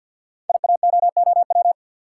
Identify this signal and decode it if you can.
Morse code, heard as a keyed tone, about 46 wpm, 700 Hz. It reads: IAOOW
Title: Code Groups - 5 Letters - 50wpm